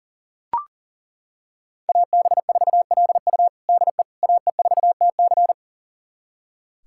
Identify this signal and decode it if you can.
Morse code, heard as a keyed tone, about 40 wpm, 700 Hz. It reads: AB4LU DE AE4TC
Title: USA Callsign DE Callsign - 5 Characters - 40wpm